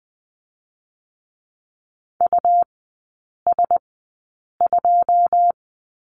Morse code, heard as a keyed tone, about 20 wpm, 700 Hz. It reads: U S 2
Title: 114 Lesson 19e - Letter w - Three Characters 20wpm